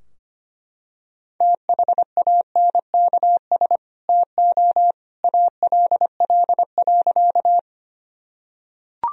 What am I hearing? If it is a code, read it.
Morse code, heard as a keyed tone, about 25 wpm, 700 Hz. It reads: THANKS TO ALL.
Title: Sentences from Top 1000 Words - Review Part 3 - 25wpm